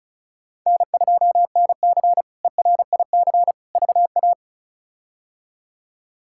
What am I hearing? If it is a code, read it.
Morse code, heard as a keyed tone, about 34 wpm, 700 Hz. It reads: N2NC ERIC VA
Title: K1USN SST QSO Practice - 35wpm